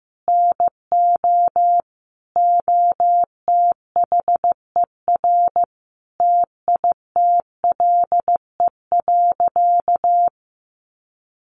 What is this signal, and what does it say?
Morse code, heard as a keyed tone, about 15 wpm, 700 Hz. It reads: NO OTHER TITLE.
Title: Sentences from Top 1200 Words 15wpm